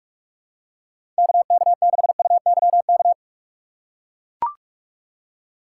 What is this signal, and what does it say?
Morse code, heard as a keyed tone, about 44 wpm, 700 Hz. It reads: KK6UYK
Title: USA Call Signs - 6 Characters - 6x Character Spacing - 45wpm